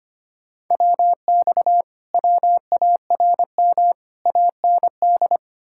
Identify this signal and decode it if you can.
Morse code, heard as a keyed tone, about 33 wpm, 700 Hz. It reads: WX WARM AND